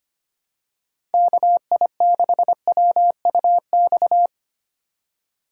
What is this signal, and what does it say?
Morse code, heard as a keyed tone, about 25 wpm, 700 Hz. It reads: KI6WUX
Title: USA Call Signs - 6 Characters - 6x Character Spacing - 25wpm